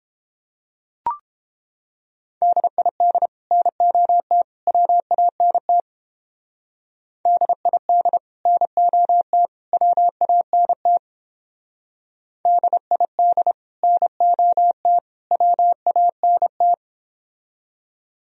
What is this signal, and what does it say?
Morse code, heard as a keyed tone, about 31 wpm, 700 Hz. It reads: DID NOT WANT DID NOT WANT DID NOT WANT